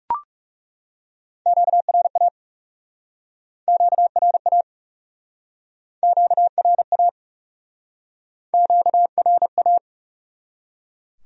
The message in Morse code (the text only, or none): QRA QRA QRA QRA